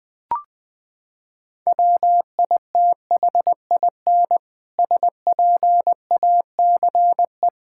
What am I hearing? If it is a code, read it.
Morse code, heard as a keyed tone, about 20 wpm, 700 Hz. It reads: WITHIN SPACE
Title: Sentences from Top 1000 Words - Review Part 5 - 20wpm